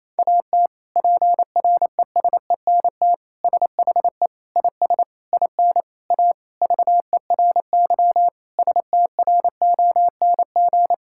Morse code, heard as a keyed tone, about 28 wpm, 700 Hz. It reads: AT PRESENT SHE IS IN A VERY STRONG
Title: Sentences from Top 1000 Words - Review Part 5 - 28wpm